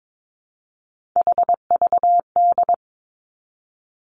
Morse code, heard as a keyed tone, about 22 wpm, 700 Hz. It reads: HVD
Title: Three Challenging Letter-Number Combinations 22wpm